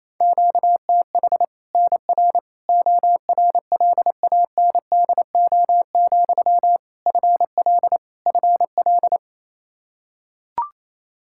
Morse code, heard as a keyed tone, about 28 wpm, 700 Hz. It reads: QTH NR ORLANDO, FL FL
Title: Practice QSO - Exchange 1 - Beginning - 28wpm